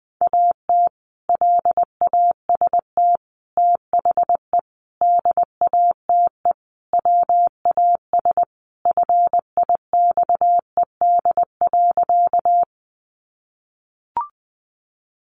Morse code, heard as a keyed tone, about 20 wpm, 700 Hz. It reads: AT LAST THE DATE WAS FIXED.
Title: Sentences from Top 1100 Words 20wpm